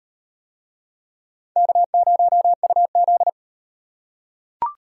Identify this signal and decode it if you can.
Morse code, heard as a keyed tone, about 36 wpm, 700 Hz. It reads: K0UZ